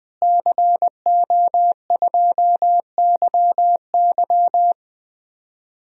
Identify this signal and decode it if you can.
Morse code, heard as a keyed tone, about 20 wpm, 700 Hz. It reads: CO2YY
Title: DXCC-100 Call Signs - 5 Characters - Repeated 3x - 20wpm